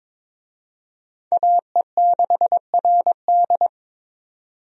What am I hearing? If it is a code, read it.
Morse code, heard as a keyed tone, about 22 wpm, 700 Hz. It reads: AE6RD
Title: USA Call Signs - 5 Characters - 22wpm